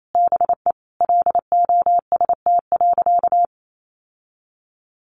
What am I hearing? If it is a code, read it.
Morse code, heard as a keyed tone, about 28 wpm, 700 Hz. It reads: BE LOST.
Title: Sentences from Top 1400 Words 28wpm